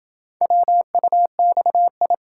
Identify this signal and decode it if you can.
Morse code, heard as a keyed tone, about 27 wpm, 700 Hz. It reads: WUXI